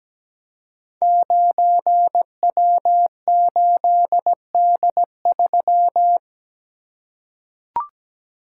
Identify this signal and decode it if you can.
Morse code, heard as a keyed tone, about 17 wpm, 700 Hz. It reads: 9W8D3
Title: Code Groups - 5 Letter:Numbers - 17wpm